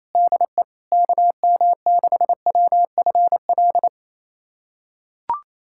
Morse code, heard as a keyed tone, about 28 wpm, 700 Hz. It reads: DE KM6WFL
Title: USA Callsign DE Callsign - 6 Characters - 28wpm